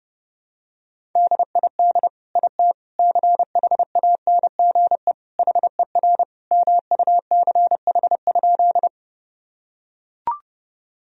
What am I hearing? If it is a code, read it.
Morse code, heard as a keyed tone, about 32 wpm, 700 Hz. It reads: DID IT CHANGE HER MUCH?
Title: Sentences from Top 1000 Words - Review Part 1 - 30wpm